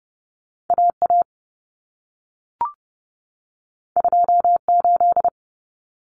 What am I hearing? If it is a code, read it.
Morse code, heard as a keyed tone, about 30 wpm, 700 Hz. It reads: AA 28
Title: Common Cut Numbers - 2-Digit 30wpm